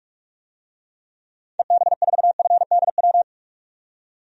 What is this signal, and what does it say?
Morse code, heard as a keyed tone, about 46 wpm, 700 Hz. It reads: EB4FDW
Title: DXCC-100 Call Signs - 6 Characters - 45wpm